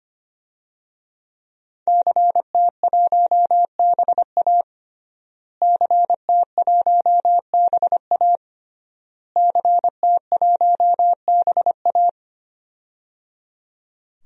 Morse code, heard as a keyed tone, about 25 wpm, 700 Hz. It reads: CT1BA CT1BA CT1BA
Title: DXCC-100 Call Signs - 5 Characters - Repeated 3x - 25wpm